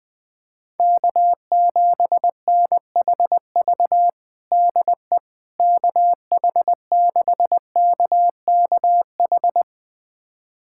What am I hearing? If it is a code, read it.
Morse code, heard as a keyed tone, about 21 wpm, 700 Hz. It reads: K7NHV DE KH6KKH
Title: USA Callsign DE Callsign 20wpm